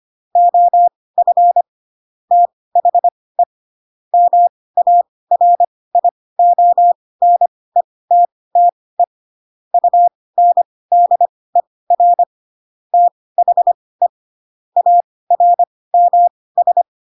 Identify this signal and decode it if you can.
Morse code, heard as a keyed tone, about 25 wpm, spaced as for 18 wpm, 700 Hz. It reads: OF THE MARIONETTE UNDER THE ARMS
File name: Pinocchio ch3 12 25 ik1hge